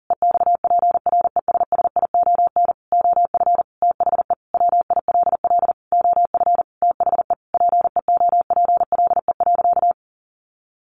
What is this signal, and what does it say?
Morse code, heard as a keyed tone, about 40 wpm, 700 Hz. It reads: EXPRESSION OF THE WILL OF THE PEOPLE.
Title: Sentences from Top 1500 Words 40wpm